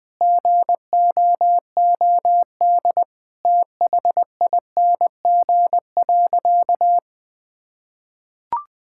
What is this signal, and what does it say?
Morse code, heard as a keyed tone, about 20 wpm, 700 Hz. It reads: GOOD THING.
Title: Sentences with 3-Word Phrases - Spoken Beginning - 20wpm